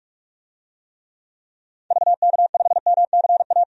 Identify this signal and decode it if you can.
Morse code, heard as a keyed tone, about 46 wpm, 700 Hz. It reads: VK5MCA
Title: DXCC-100 Call Signs - 6 Characters - 45wpm